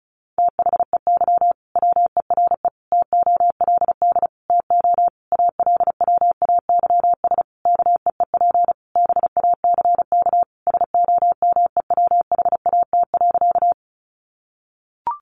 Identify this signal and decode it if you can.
Morse code, heard as a keyed tone, about 34 wpm, 700 Hz. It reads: THEY WERE TOLD TO ALWAYS KEEP BACK SOMEWHAT.
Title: Sentences from Top 1300 Words 35wpm